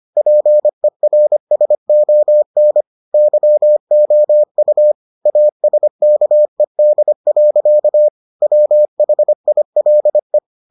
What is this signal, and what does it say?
Morse code, heard as a keyed tone, about 25 wpm, 600 Hz. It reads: PERSON YOU ASKED. WHILE